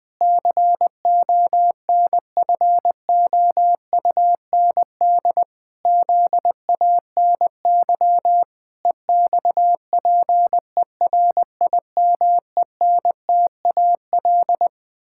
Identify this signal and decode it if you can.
Morse code, heard as a keyed tone, about 20 wpm, 700 Hz. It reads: CONFOUND ZANY EXPERIMENTAL
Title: Pangram Challenge v2 - 20wpm